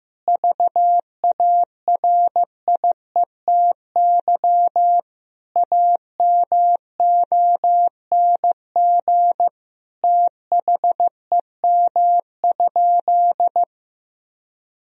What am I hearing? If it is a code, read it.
Morse code, heard as a keyed tone, about 16 wpm, 700 Hz. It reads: VARIETY AMONG THEM?
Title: Sentences from Top 1000 Words - Review Part 4 - 15wpm